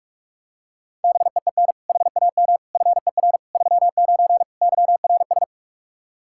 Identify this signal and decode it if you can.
Morse code, heard as a keyed tone, about 48 wpm, 700 Hz. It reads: BEEN HAM FER 39 YRS
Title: Practice QSO - Exchange 3 - More Info - 45wpm